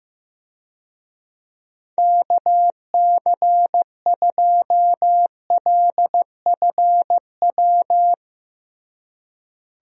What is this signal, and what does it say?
Morse code, heard as a keyed tone, about 16 wpm, 700 Hz. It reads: KC2LFW
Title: USA Call Signs - 6 Characters - Repeated 2x - 15wpm